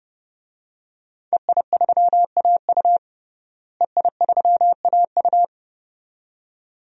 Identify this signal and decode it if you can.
Morse code, heard as a keyed tone, about 30 wpm, 700 Hz. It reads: EI3AU EI3AU
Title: DXCC-100 Call Signs - 5 Characters - Repeated 2x - 30wpm